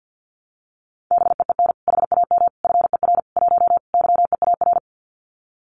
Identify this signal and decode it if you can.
Morse code, heard as a keyed tone, about 50 wpm, 700 Hz. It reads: BEEN HAM FER 1 YEAR
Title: Common QSO Phrases Encore 50wpm